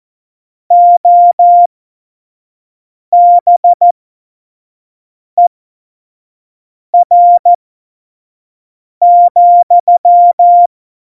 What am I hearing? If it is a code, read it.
Morse code, heard as a keyed tone, about 14 wpm, spaced as for 5 wpm, 700 Hz. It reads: OBER,
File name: Morsekurs / Witze / Witz 24. September 2019, 25 F70